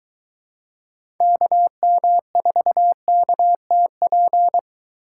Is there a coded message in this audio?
Yes: Morse code from 1.2 to 4.6 s, heard as a keyed tone, about 23 wpm, 700 Hz. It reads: KM4KTP